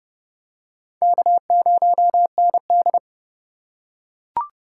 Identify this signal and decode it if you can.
Morse code, heard as a keyed tone, about 30 wpm, 700 Hz. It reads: K0ND
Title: USA Call Signs - 4 Characters - 3x Character Spacing - 30wpm